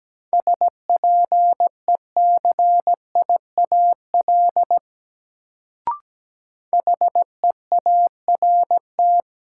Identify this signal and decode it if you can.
Morse code, heard as a keyed tone, about 17 wpm, 700 Hz. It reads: SPECIAL HEART
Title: Sentences from Top 1200 Words - New Word Drill Alt - 17wpm